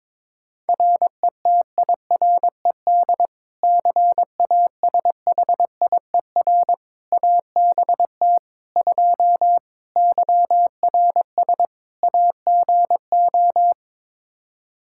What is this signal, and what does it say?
Morse code, heard as a keyed tone, about 22 wpm, 700 Hz. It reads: RETIRED CASHIER ABT 2 YRS AGO
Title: K1USN SST QSO Practice 22wpm